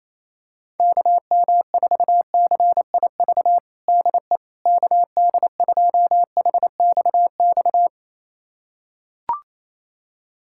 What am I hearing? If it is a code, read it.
Morse code, heard as a keyed tone, about 28 wpm, 700 Hz. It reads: KM4CIV DE KD2HXX
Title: USA Callsign DE Callsign 28wpm